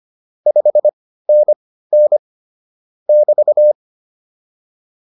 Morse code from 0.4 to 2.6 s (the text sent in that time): 5NN